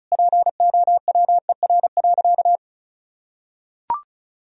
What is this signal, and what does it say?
Morse code, heard as a keyed tone, about 34 wpm, 700 Hz. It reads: POWER.
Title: Sentences from Top 1000 Words - Review Part 2 - 35wpm